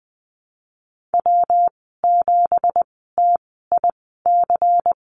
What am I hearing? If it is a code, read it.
Morse code, heard as a keyed tone, about 19 wpm, spaced as for 14 wpm, 700 Hz. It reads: W7TIC
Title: 231 Lesson 37g - Letter x - Call Signs 20wpm